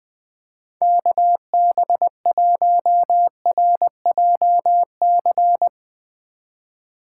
Morse code, heard as a keyed tone, about 20 wpm, 700 Hz. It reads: KB1RJC
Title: 2023 SOTA Chaser Call Signs - 20wpm